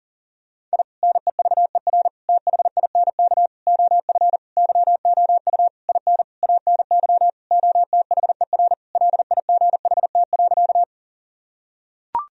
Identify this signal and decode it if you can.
Morse code, heard as a keyed tone, about 40 wpm, 700 Hz. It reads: I NEVER THINK OF YOU IN ANY OTHER LIGHT.
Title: Sentences from Top 1000 Words - Review Part 1 - 40wpm